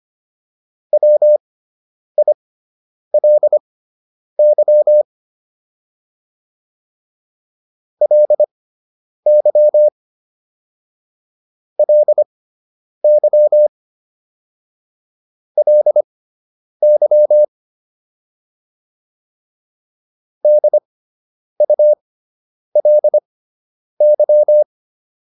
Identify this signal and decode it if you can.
Morse code, heard as a keyed tone, about 25 wpm, spaced as for 9 wpm, 600 Hz. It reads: WILY LY LY LY DULY